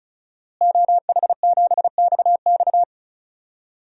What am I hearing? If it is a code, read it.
Morse code, heard as a keyed tone, about 36 wpm, 700 Hz. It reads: OH7XX
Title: DXCC-100 Call Signs - 5 Characters - Repeated 2x - 35wpm